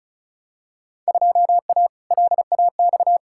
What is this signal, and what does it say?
Morse code, heard as a keyed tone, about 36 wpm, 700 Hz. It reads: 2A LAX
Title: Field Day Exchanges - 35wpm